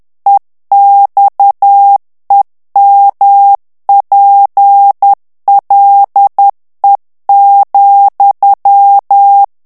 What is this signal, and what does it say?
Morse code, heard as a keyed tone, about 10 wpm, 800 Hz. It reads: EXEMPLE,
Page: Français - texte en clair 1 - vitesse : 10 mots minute